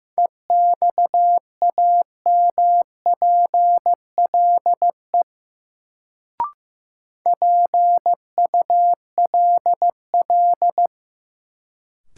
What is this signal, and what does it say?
Morse code, heard as a keyed tone, about 15 wpm, 700 Hz. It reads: EXAMPLE PULL
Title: Sentences from Top 400 Words - New Word Drill - 15wpm